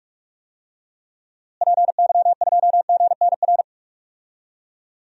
Morse code, heard as a keyed tone, about 47 wpm, 700 Hz. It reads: PY2GNR
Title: DXCC-100 Call Signs - 6 Characters - Repeated 2x - 45wpm